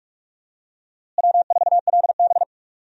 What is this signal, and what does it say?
Morse code, heard as a keyed tone, about 46 wpm, 700 Hz. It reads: W4LB
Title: USA Call Signs - 4 Characters - Repeated 2x - 45wpm